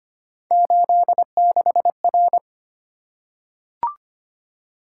Morse code, heard as a keyed tone, about 25 wpm, 700 Hz. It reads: 86R
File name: Three Letter-Number Combinations - 25wpm